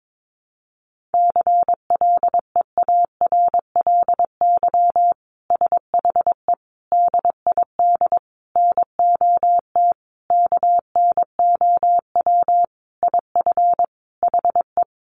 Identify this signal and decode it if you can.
Morse code, heard as a keyed tone, about 22 wpm, 700 Hz. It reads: CLEARLY SHE DID NOT KNOW IF HE
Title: Sentences from Top 1300 Words 22wpm